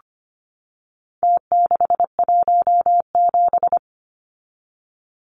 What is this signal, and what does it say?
Morse code, heard as a keyed tone, about 25 wpm, 700 Hz. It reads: T617
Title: Numbers - Contest Style - 4-Digit 25wpm